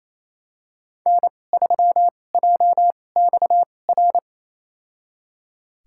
Morse code, heard as a keyed tone, about 28 wpm, spaced as for 20 wpm, 700 Hz. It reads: N3JXR
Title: USA Call Signs - 5 Characters - 2x Character Spacing - 28wpm